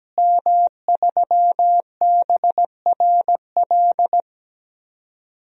Morse code, heard as a keyed tone, about 17 wpm, 700 Hz. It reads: M3BRL
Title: DXCC-100 Call Signs - 5 Characters - Repeated 3x - 17wpm